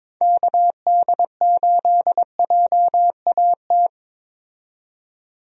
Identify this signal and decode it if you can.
Morse code, heard as a keyed tone, about 22 wpm, 700 Hz. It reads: KD8JAT
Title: USA Call Signs - 6 Characters - 22wpm